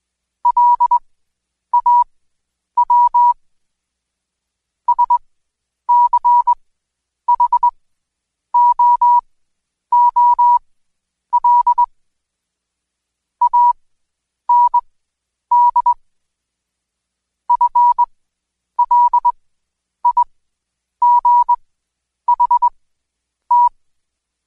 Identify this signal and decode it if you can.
Morse code, heard as a keyed tone, about 20 wpm, spaced as for 9 wpm, 970 Hz. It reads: LAW SCHOOL AND FLIGHT
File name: Code Quick / AudioDownld2 / Seg116-Day21-DayDrill 21